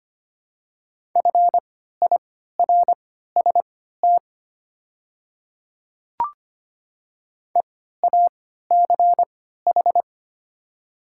Morse code, heard as a keyed tone, about 25 wpm, spaced as for 14 wpm, 700 Hz. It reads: FIRST EACH
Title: 3x Character Spacing - Top 300 Words - 25wpm